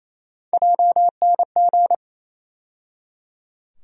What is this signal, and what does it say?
Morse code, heard as a keyed tone, about 28 wpm, 700 Hz. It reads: JNG
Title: USA Call Sign Postfixes - 3 Letters 28wpm